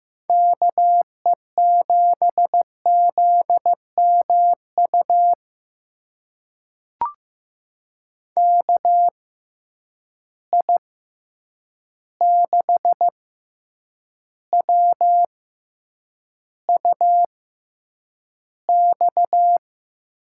Morse code, heard as a keyed tone, about 15 wpm, spaced as for 8 wpm, 700 Hz. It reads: KE7ZMU KI6WUX
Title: USA Call Signs - 6 Characters - 6x Character Spacing - 15wpm